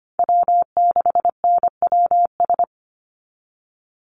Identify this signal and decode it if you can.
Morse code, heard as a keyed tone, about 25 wpm, 700 Hz. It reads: W6NWS
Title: Call Signs Repeated 2x - 25wpm